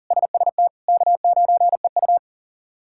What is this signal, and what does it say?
Morse code, heard as a keyed tone, about 40 wpm, 700 Hz. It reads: SST K9EU
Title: K1USN SST QSO Practice - 40wpm